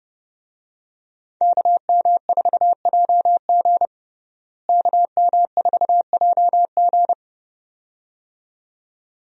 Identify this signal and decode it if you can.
Morse code, heard as a keyed tone, about 32 wpm, 700 Hz. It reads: KM4JG KM4JG
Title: USA Call Signs - 5 Characters - Repeated 2x - 30wpm